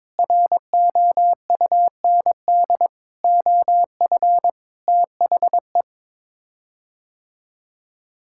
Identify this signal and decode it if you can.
Morse code, heard as a keyed tone, about 22 wpm, 700 Hz. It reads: ROUND OF THE
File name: practice / 22wpm / Sets of 6 Words - 22wpm